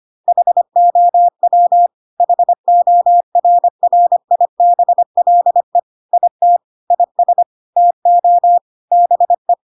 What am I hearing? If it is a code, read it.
Morse code, heard as a keyed tone, about 25 wpm, 700 Hz. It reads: HOW HORRIBLE IT IS TO BE
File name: Pinocchio / Pinocchio ch5 25 25 ik1hge